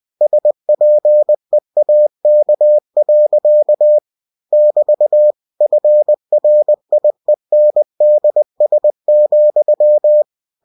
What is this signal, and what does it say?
Morse code, heard as a keyed tone, about 20 wpm, 600 Hz. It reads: SPEAK. <BT> FRIENDS,
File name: cw-books / winnie-the-pooh / winnie-20wpm-page-0107